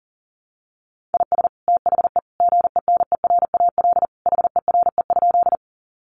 Morse code, heard as a keyed tone, about 37 wpm, 700 Hz. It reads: IS THE GENERAL HERE?
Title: Sentences from Top 600 Words 40wpm